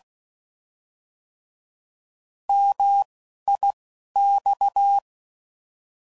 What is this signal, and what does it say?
Morse code, heard as a keyed tone, about 16 wpm, 800 Hz. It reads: M I X
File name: Mixed / Contrasts / 3LtrContrasts